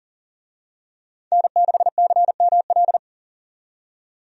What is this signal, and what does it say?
Morse code, heard as a keyed tone, about 40 wpm, 700 Hz. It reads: N6CML
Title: USA Call Signs - 5 Characters - 40wpm